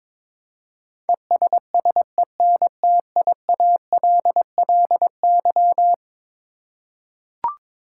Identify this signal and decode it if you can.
Morse code, heard as a keyed tone, about 22 wpm, 700 Hz. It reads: ESSENTIALLY
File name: practice / 22wpm / Eleven Letter Words - 22wpm